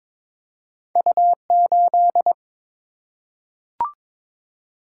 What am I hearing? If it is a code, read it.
Morse code, heard as a keyed tone, about 22 wpm, 700 Hz. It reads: U8
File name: practice / 22wpm / Two Letter-Number Combinations - 22wpm